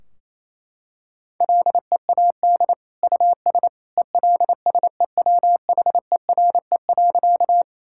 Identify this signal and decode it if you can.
Morse code, heard as a keyed tone, about 27 wpm, 700 Hz. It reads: LEAD US ELSEWHERE.
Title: Sentences from Top 1600 Words 28wpm